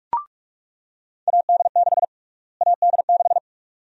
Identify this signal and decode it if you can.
Morse code, heard as a keyed tone, about 42 wpm, 700 Hz. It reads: AD6 AD6
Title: USA Call Sign Prefixes - Repeated 2x - 45wpm